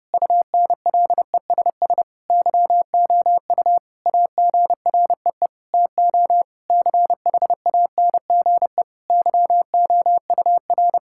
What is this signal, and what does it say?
Morse code, heard as a keyed tone, about 30 wpm, 700 Hz. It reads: UNLESS YOU AGREE TO CHANGE YOUR
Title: Sentences from Top 1000 Words - Review Part 6 - 30wpm